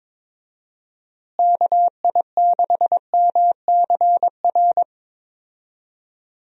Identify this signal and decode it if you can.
Morse code, heard as a keyed tone, about 21 wpm, 700 Hz. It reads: KI6MCR
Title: USA Call Signs - 6 Characters - Repeated 2x - 22wpm